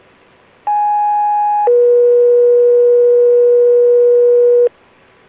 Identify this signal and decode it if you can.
Two-tone page, heard as a one-second tone, then a three-second tone, about 830 then 480 Hz. It is heard from 0.7 to 4.7 s.